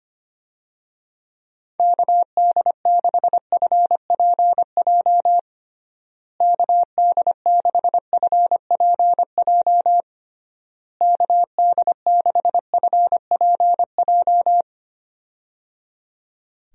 Morse code, heard as a keyed tone, about 25 wpm, 700 Hz. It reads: KD6FPJ KD6FPJ KD6FPJ
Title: USA Call Signs - 6 Characters - Repeated 3x - 25wpm